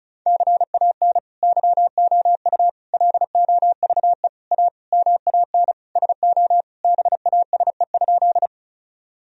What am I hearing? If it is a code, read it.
Morse code, heard as a keyed tone, about 36 wpm, 700 Hz. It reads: CAN YOU LOVE A MAN SO BASE?
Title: Sentences from Top 1000 Words - Review Part 1 - 35wpm